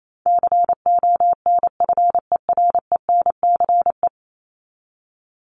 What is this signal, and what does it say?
Morse code, heard as a keyed tone, about 28 wpm, 700 Hz. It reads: CONFERENCE
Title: Top 1751-2000 Words 28wpm